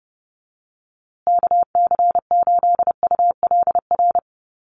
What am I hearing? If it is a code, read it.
Morse code, heard as a keyed tone, about 30 wpm, 700 Hz. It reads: KC8ULR
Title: USA Call Signs Repeated 2x 30wpm